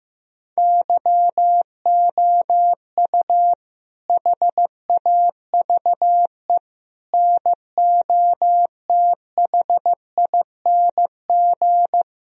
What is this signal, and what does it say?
Morse code, heard as a keyed tone, about 15 wpm, 700 Hz. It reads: YOU HAVE NOTHING
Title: Sentences from Top 1000 Words - Review Part 1 - 15wpm